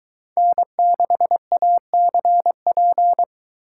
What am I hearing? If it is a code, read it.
Morse code, heard as a keyed tone, about 22 wpm, 700 Hz. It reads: N6ACP